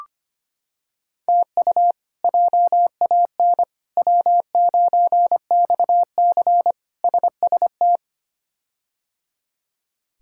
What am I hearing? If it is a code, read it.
Morse code, heard as a keyed tone, about 25 wpm, 700 Hz. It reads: TU JAN W9XC SST
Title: K1USN SST QSO Practice - 25wpm